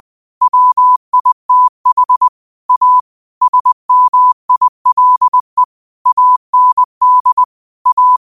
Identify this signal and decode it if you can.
Morse code, heard as a keyed tone, about 20 wpm, 1000 Hz. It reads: WITH A SMILE AND A